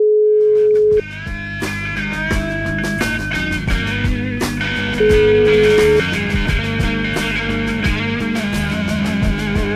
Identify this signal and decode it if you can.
Ringback tone (Europe, 425 Hz), with noise.